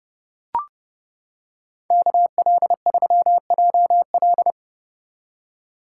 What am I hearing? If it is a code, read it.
Morse code, heard as a keyed tone, about 30 wpm, 700 Hz. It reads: KL3JL